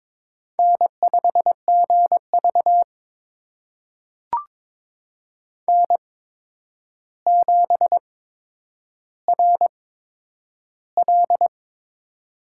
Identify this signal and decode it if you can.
Morse code, heard as a keyed tone, about 22 wpm, spaced as for 9 wpm, 700 Hz. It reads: N5GV N7RL
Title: USA Call Signs - 4 Characters - 8x Character Spacing - 22wpm